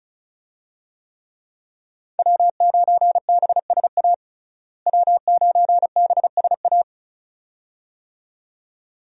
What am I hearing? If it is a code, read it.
Morse code, heard as a keyed tone, about 36 wpm, 700 Hz. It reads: W9BSA W9BSA
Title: USA Call Signs - 5 Characters - Repeated 2x - 35wpm